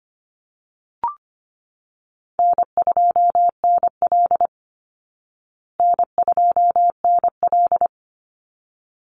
Morse code, heard as a keyed tone, about 25 wpm, 700 Hz. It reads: N2NL N2NL
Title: Easy Call Signs Repeated 2x - 25wpm